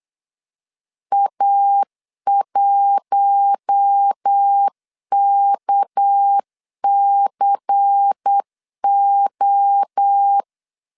Morse code, heard as a keyed tone, about 8 wpm, 800 Hz. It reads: A1KCO
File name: 8wpm